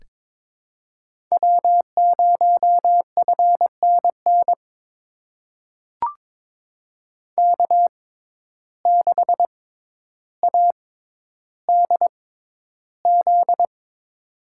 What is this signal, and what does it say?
Morse code, heard as a keyed tone, about 22 wpm, spaced as for 11 wpm, 700 Hz. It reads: W0FNN K6ADZ
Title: USA Call Signs - 5 Characters - 6x Character Spacing - 22wpm